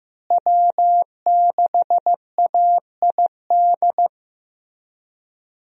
Morse code, heard as a keyed tone, about 15 wpm, 700 Hz. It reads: W6AID
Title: USA Call Signs - 5 Characters - Repeated 3x - 15wpm